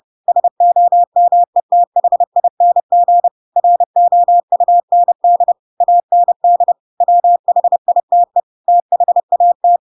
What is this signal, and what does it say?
Morse code, heard as a keyed tone, about 30 wpm, 700 Hz. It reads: SOMETHING ROUND AND WHITE THAT
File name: Pinocchio / Pinocchio ch5 30 30 ik1hge